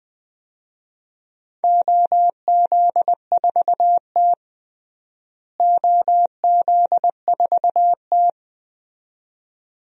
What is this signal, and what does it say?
Morse code, heard as a keyed tone, about 21 wpm, 700 Hz. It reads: OZ4T OZ4T
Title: DXCC-100 Call Signs - 4 Characters - Repeated 2x - 20wpm